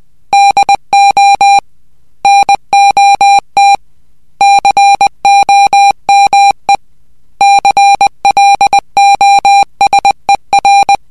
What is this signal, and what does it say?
Morse code, heard as a keyed tone, about 20 wpm, 800 Hz. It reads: DO NOT COME CLOSER